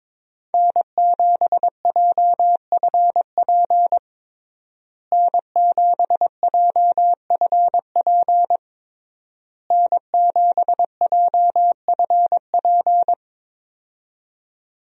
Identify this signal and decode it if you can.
Morse code, heard as a keyed tone, about 22 wpm, 700 Hz. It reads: N7JFP N7JFP N7JFP
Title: USA Call Signs - 5 Characters - Repeated 3x - 22wpm